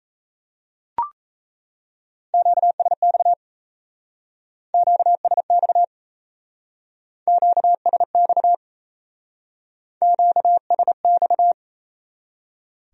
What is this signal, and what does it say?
Morse code, heard as a keyed tone, about 34 wpm, 700 Hz. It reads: QSX QSX QSX QSX
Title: Speed-Racing Q-Codes - 28wpm